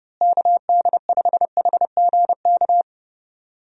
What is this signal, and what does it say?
Morse code, heard as a keyed tone, about 30 wpm, 700 Hz. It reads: KD5HGK
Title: USA Call Signs - 6 Characters - Repeated 3x - 30wpm